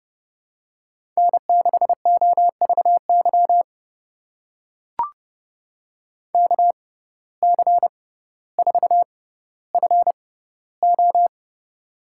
Morse code, heard as a keyed tone, about 30 wpm, spaced as for 15 wpm, 700 Hz. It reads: N6OVY KC4FO
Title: USA Call Signs - 6x Character Spacing - 30wpm